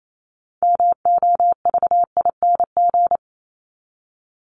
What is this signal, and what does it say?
Morse code, heard as a keyed tone, about 28 wpm, 700 Hz. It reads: MOVING
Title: ING Suffix - Top 100 Words with Base Words 28wpm